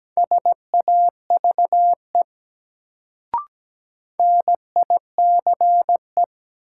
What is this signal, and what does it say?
Morse code, heard as a keyed tone, about 17 wpm, 700 Hz. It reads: SAVE NICE
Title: Sentences from Top 1100 Words - New Word Drill Alt - 17wpm